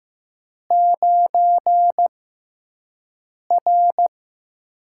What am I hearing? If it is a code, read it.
Morse code, heard as a keyed tone, about 15 wpm, spaced as for 5 wpm, 700 Hz. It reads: 9R
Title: USA Call Signs - 4 Characters - 6x Character Spacing - 15wpm